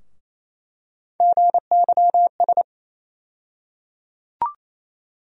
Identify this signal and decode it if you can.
Morse code, heard as a keyed tone, about 28 wpm, 700 Hz. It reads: GYS
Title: USA Call Sign Postfixes - 3 Letters 28wpm